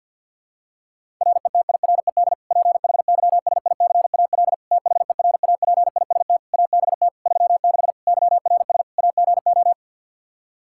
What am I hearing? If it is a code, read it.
Morse code, heard as a keyed tone, about 52 wpm, 700 Hz. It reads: RETIRED PHYSICAL THERAPIST ABT 36 YRS AGO
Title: Practice QSO - Exchange 3 - More Info - 50wpm